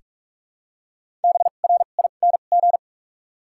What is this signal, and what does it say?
Morse code, heard as a keyed tone, about 46 wpm, spaced as for 32 wpm, 700 Hz. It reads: BRING
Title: 2x Character Spacing - Top 500 Words - 45wpm